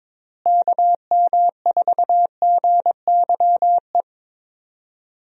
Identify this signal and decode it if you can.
Morse code, heard as a keyed tone, about 22 wpm, 700 Hz. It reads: KM4GYE
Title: USA Call Signs - 6 Characters - Repeated 3x - 22wpm